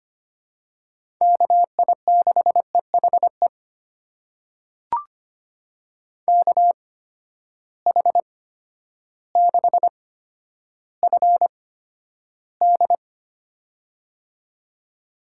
Morse code, heard as a keyed tone, about 25 wpm, spaced as for 11 wpm, 700 Hz. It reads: KI6EHE KH6FD
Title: USA Call Signs - 8x Character Spacing - 25wpm